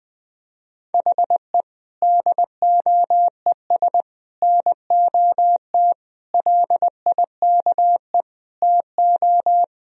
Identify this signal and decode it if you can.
Morse code, heard as a keyed tone, about 20 wpm, 700 Hz. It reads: HE DOES NOT LIKE TO
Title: Sentences from Top 1000 Words - Review Part 3 - 20wpm